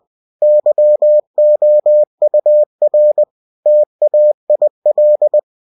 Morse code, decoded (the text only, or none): YOUR TAIL